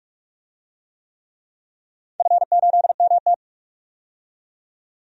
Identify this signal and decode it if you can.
Morse code, heard as a keyed tone, about 43 wpm, 700 Hz. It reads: F8MT
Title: DXCC-100 Call Signs - 4 Characters - 45wpm